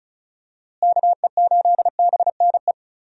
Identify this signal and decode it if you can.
Morse code, heard as a keyed tone, about 36 wpm, 700 Hz. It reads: KE8BNE
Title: USA Call Signs - 6 Characters - Repeated 2x - 35wpm